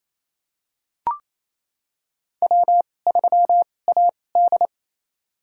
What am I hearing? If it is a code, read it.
Morse code, heard as a keyed tone, about 28 wpm, spaced as for 20 wpm, 700 Hz. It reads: W3AD
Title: USA Call Signs - 2x Character Spacing - 28wpm